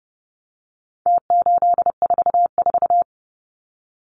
Morse code, heard as a keyed tone, about 30 wpm, 700 Hz. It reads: T844
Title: Numbers - Contest Style - 4-Digit 30wpm